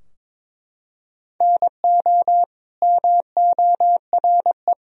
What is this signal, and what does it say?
Morse code, heard as a keyed tone, about 21 wpm, 700 Hz. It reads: NO MORE
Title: Sentences from Top 1000 Words - Review Part 2 - 22wpm